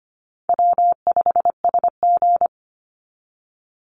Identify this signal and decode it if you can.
Morse code, heard as a keyed tone, about 25 wpm, 700 Hz. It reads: W5SG
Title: Easy Call Signs - 25wpm